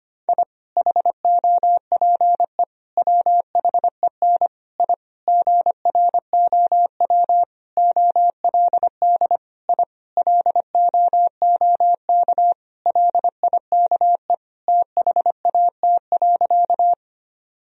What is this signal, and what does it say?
Morse code, heard as a keyed tone, about 25 wpm, 700 Hz. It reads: I HOPE WHEN I GROW OLD I LOOK LIKE THAT.
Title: Sentences from Top 1000 Words - Review Part 1 - 25wpm